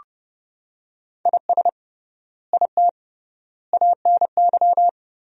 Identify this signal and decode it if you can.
Morse code, heard as a keyed tone, about 30 wpm, 700 Hz. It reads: IS IT ANY
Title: 3x Word-Spacing - Sentences from Top 200 Words - 30wpm